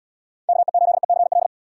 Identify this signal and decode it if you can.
Morse code, heard as a keyed tone, about 71 wpm, 700 Hz. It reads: KE8ECL